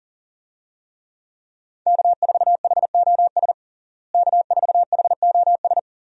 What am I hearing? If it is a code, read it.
Morse code, heard as a keyed tone, about 40 wpm, 700 Hz. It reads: K4HOS K4HOS
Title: USA Call Signs - 5 Characters - Repeated 3x - 40wpm